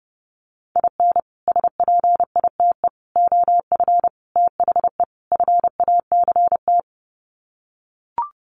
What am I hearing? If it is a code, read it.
Morse code, heard as a keyed tone, about 30 wpm, 700 Hz. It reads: IN SPITE OF THE FACT
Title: Sets of 5 Words Encore2 30wpm